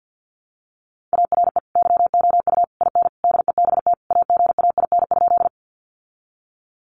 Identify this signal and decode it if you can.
Morse code, heard as a keyed tone, about 50 wpm, 700 Hz. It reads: ARE YOU IN DEBT AGAIN?
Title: Sentences from Top 1300 Words 50wpm